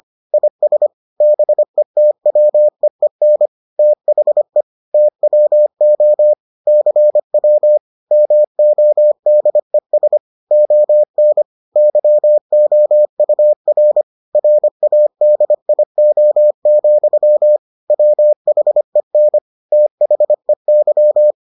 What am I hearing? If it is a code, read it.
Morse code, heard as a keyed tone, about 25 wpm, 600 Hz. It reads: IS BETWEEN THE TWO CW MODES ON YOUR RADIO, WHEN THEY